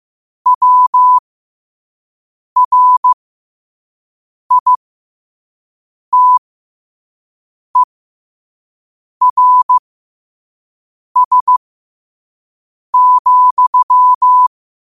Morse code, heard as a keyed tone, about 15 wpm, spaced as for 5 wpm, 1000 Hz. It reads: WRITERS,
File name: qotdfiles / 5wpm